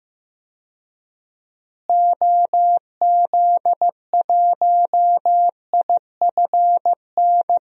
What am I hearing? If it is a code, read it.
Morse code, heard as a keyed tone, about 15 wpm, 700 Hz. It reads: OZ1IFN
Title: DXCC-100 Call Signs - 6 Characters - Repeated 3x - 15wpm